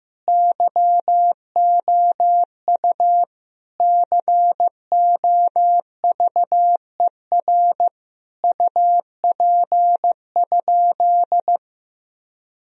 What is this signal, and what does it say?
Morse code, heard as a keyed tone, about 15 wpm, 700 Hz. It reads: YOU COVER UP?
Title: Sentences from Top 1000 Words - Review Part 1 - 15wpm